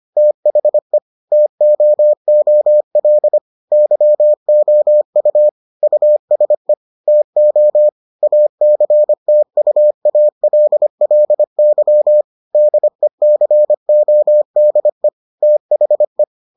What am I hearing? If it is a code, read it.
Morse code, heard as a keyed tone, about 25 wpm, 600 Hz. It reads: THE TOOL YOU USE TO ACTUALLY DECODE THE